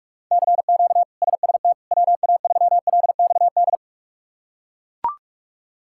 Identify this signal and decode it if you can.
Morse code, heard as a keyed tone, about 41 wpm, 700 Hz. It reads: CQ SST WA3LXD
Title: K1USN SST QSO Practice - 45wpm